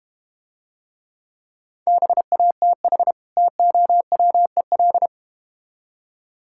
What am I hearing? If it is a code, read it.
Morse code, heard as a keyed tone, about 32 wpm, 700 Hz. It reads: BATH TOWEL